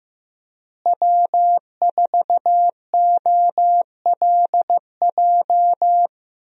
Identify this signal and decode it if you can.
Morse code, heard as a keyed tone, about 15 wpm, 700 Hz. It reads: W4OLJ
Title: USA Call Signs - 5 Characters - Repeated 2x - 15wpm